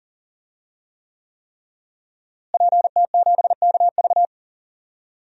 Morse code, heard as a keyed tone, about 37 wpm, 700 Hz. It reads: PT7KV
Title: DXCC-100 Call Signs - 5 Characters - Repeated 3x - 40wpm